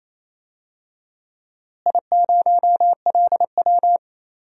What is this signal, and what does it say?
Morse code, heard as a keyed tone, about 27 wpm, 700 Hz. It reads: I0LW
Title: DXCC-100 Call Signs - 4 Characters - Repeated 3x - 28wpm